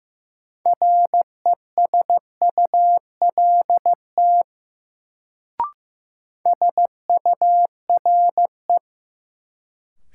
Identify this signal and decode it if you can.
Morse code, heard as a keyed tone, about 15 wpm, 700 Hz. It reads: RESULT SURE
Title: Sets of 3 Words - New Word Drill - 15wpm